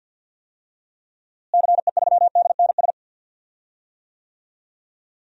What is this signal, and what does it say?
Morse code, heard as a keyed tone, about 52 wpm, 700 Hz. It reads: CE3DNS
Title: DXCC-100 Call Signs - 6 Characters - Repeated 2x - 50wpm